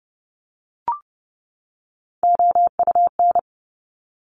Morse code, heard as a keyed tone, about 30 wpm, 690 Hz. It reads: OUN
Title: Top Three Letter Combinations in Words 30wpm